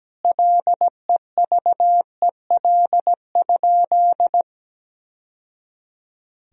Morse code, heard as a keyed tone, about 17 wpm, 700 Hz. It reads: LEVEL?
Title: Sentences from Top 1000 Words - Review Part 4 - 17wpm